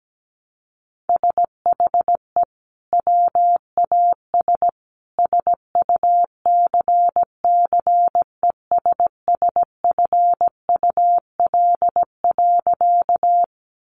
Morse code, heard as a keyed tone, about 17 wpm, 700 Hz. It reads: SHE WAS SUCCESSFUL.
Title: Sentences from Top 900 Words 17wpm